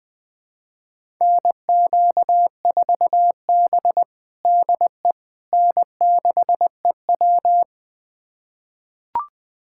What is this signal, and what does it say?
Morse code, heard as a keyed tone, about 20 wpm, 700 Hz. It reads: NQ4B DE N6EW
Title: USA Callsign DE Callsign - 4 Characters - 20wpm